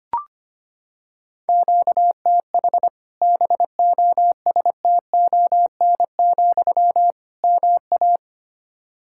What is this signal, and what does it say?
Morse code, heard as a keyed tone, about 25 wpm, 700 Hz. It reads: QTH BOSTON, MA
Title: Practice QSO - Exchange 1 - Beginning - 25wpm